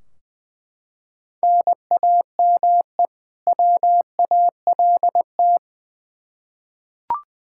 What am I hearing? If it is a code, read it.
Morse code, heard as a keyed tone, about 20 wpm, 700 Hz. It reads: NAME WALT
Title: Practice QSO - Exchange 1 - Beginning - 20wpm